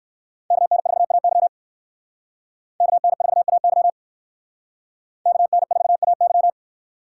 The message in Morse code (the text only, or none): KN4AY KN4AY KN4AY